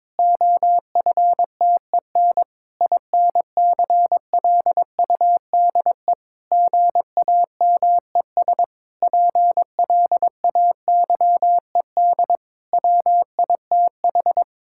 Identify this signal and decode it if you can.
Morse code, heard as a keyed tone, about 22 wpm, 700 Hz. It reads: OFTEN INCLUDE GAMES PLAYED WITH
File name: Difficult Sentences with Hints - 22wpm